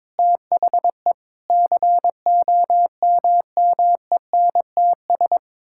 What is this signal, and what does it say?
Morse code, heard as a keyed tone, about 22 wpm, 700 Hz. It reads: THE COMMENTS
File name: practice / 22wpm / Sets of 8 Words - 22wpm